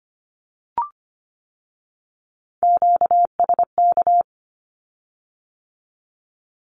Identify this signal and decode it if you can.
Morse code, heard as a keyed tone, about 25 wpm, 700 Hz. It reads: QSK
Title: Q-Codes with Spoken Meaning - 25wpm